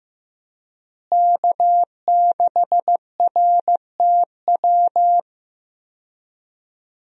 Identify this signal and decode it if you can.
Morse code, heard as a keyed tone, about 15 wpm, 700 Hz. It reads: K6RTW
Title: USA Call Signs - 5 Characters - Repeated 2x - 15wpm